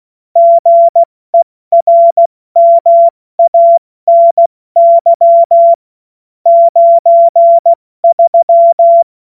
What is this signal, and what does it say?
Morse code, heard as a keyed tone, about 16 wpm, 680 Hz. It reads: GERMANY 93
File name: Morsekurs / Witze / Witz 02 April 2024 70 F80